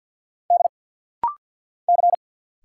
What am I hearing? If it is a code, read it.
Morse code, heard as a keyed tone, about 49 wpm, 700 Hz. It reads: D C
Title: Single Letter-Number Alt - Rapid-Fire 50wpm